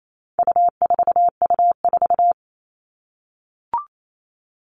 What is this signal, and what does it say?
Morse code, heard as a keyed tone, about 28 wpm, 700 Hz. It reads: U4U4
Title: All Cut Numbers - 4-Digit 28wpm